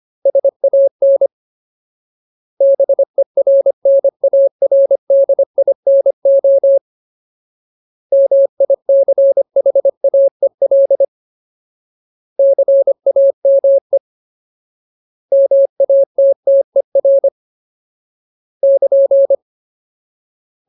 Morse code, heard as a keyed tone, about 25 wpm, 550 Hz. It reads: SAN BERNARDINO MICHAEL CAME MATTER <KN>
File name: Ultimate Word List / 25WPM Ultimate